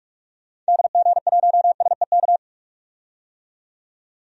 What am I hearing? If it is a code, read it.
Morse code, heard as a keyed tone, about 43 wpm, 700 Hz. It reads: DG1SEK
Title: DXCC-100 Call Signs - 6 Characters - 45wpm